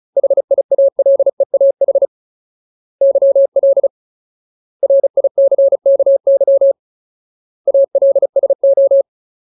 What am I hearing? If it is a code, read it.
Morse code, heard as a keyed tone, about 34 wpm, 550 Hz. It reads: HIALEAH YL RICKY ALSO
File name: Ultimate Word List / 35WPM Ultimate